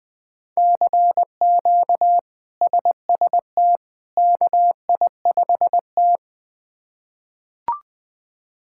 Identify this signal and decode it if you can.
Morse code, heard as a keyed tone, about 20 wpm, 700 Hz. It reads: CQ SST KI5T
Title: K1USN SST QSO Practice - 20wpm